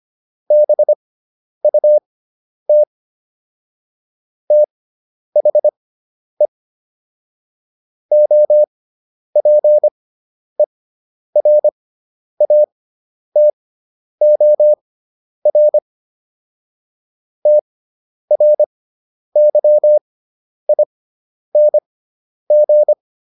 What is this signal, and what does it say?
Morse code, heard as a keyed tone, about 25 wpm, spaced as for 10 wpm, 600 Hz. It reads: BUT THE OPERATOR TRYING